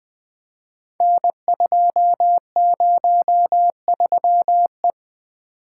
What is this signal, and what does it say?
Morse code, heard as a keyed tone, about 20 wpm, 700 Hz. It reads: N203E
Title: Code Groups - 5 Letter:Numbers - 20wpm